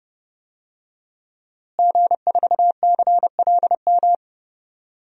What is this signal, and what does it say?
Morse code, heard as a keyed tone, about 30 wpm, 700 Hz. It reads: G4CLM
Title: DXCC-100 Call Signs - 5 Characters - Repeated 3x - 30wpm